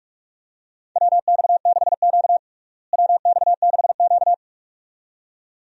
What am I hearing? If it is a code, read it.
Morse code, heard as a keyed tone, about 44 wpm, 700 Hz. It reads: WX6Q WX6Q
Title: USA Call Signs - 4 Characters - Repeated 2x - 45wpm